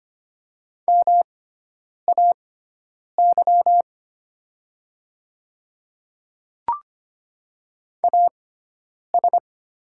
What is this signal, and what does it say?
Morse code, heard as a keyed tone, about 25 wpm, spaced as for 9 wpm, 700 Hz. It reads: MAY AS
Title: 6x Character Spacing - Top 100 Words - 25wpm